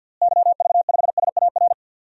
Morse code, heard as a keyed tone, about 52 wpm, 700 Hz. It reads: YV5SAR
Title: DXCC-100 Call Signs - 6 Characters - Repeated 3x - 50wpm